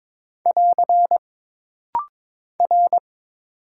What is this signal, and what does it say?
Morse code, heard as a keyed tone, about 22 wpm, 700 Hz. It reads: <AR> R
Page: This rapid-fire timing is geared to push your ICR (Instant Character Recognition) skills.